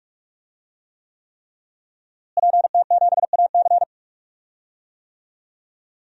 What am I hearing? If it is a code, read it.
Morse code, heard as a keyed tone, about 41 wpm, 700 Hz. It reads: PT7AC
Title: DXCC-100 Call Signs - 5 Characters - Repeated 3x - 45wpm